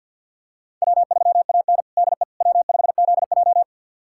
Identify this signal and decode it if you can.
Morse code, heard as a keyed tone, about 52 wpm, 700 Hz. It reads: W3AN DE W5ZJ
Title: USA Callsign DE Callsign - 4 Characters - 50wpm